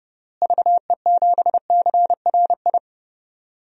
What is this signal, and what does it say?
Morse code, heard as a keyed tone, about 30 wpm, 700 Hz. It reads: VE7CRI
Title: DXCC-100 Call Signs - 6 Characters - Repeated 3x - 30wpm